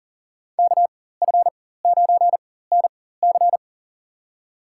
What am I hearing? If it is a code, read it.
Morse code, heard as a keyed tone, about 40 wpm, spaced as for 19 wpm, 700 Hz. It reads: KF9NC
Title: USA Call Signs - 5 Characters - 4x Character Spacing - 40wpm